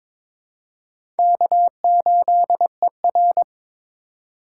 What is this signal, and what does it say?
Morse code, heard as a keyed tone, about 22 wpm, 700 Hz. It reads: K8ER
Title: USA Call Signs - 4 Characters - 2x Character Spacing - 22wpm